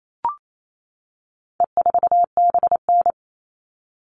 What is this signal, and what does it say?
Morse code, heard as a keyed tone, about 28 wpm, 700 Hz. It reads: E4BN
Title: All Cut Numbers - 4-Digit 28wpm